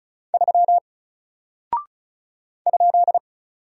Morse code, heard as a keyed tone, about 36 wpm, 700 Hz. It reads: 3 ?
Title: Single Character (Letter:Number:Punctuation:Prosign) Alt 35wpm